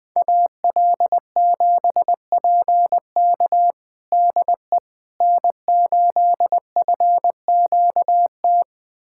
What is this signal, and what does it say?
Morse code, heard as a keyed tone, about 20 wpm, 700 Hz. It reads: AL7PK DE N8FQT
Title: USA Callsign DE Callsign - 5 Characters - 20wpm